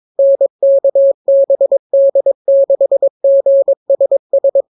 Morse code, heard as a keyed tone, about 22 wpm, 550 Hz. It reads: NKBD6GSS